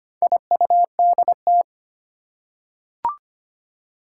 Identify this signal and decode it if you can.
Morse code, heard as a keyed tone, about 25 wpm, 700 Hz. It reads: IUDT
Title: Four Challenging Letter-Number Combinations 25wpm